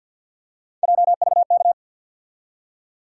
Morse code, heard as a keyed tone, about 47 wpm, 700 Hz. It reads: JVK
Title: USA Call Sign Postfixes - 3 Letters 50wpm